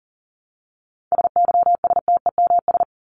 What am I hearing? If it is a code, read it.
Morse code, heard as a keyed tone, about 40 wpm, 700 Hz. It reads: SYSTEMS
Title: ES:S Suffix - Top 100 Words with Base Words 40wpm